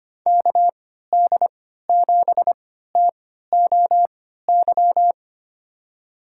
Morse code, heard as a keyed tone, about 25 wpm, spaced as for 14 wpm, 700 Hz. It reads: KD7TOY
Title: USA Call Signs - 6 Characters - 3x Character Spacing - 25wpm